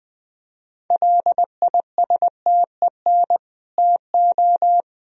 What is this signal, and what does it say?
Morse code, heard as a keyed tone, about 20 wpm, 700 Hz. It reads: LISTEN TO
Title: Sentences from Top 400 Words and Abbreviations 20wpm